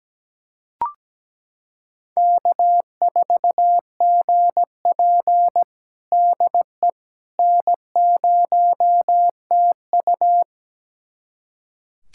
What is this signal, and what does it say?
Morse code, heard as a keyed tone, about 17 wpm, 700 Hz. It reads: K4GP DE N0TU
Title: USA Callsign DE Callsign - 4 Characters - 17wpm